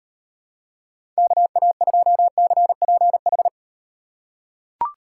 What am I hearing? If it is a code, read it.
Morse code, heard as a keyed tone, about 36 wpm, 700 Hz. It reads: KA2CPH